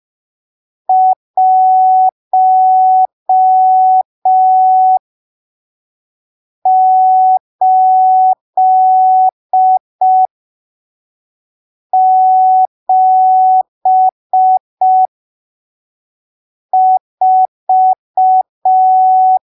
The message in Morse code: 1874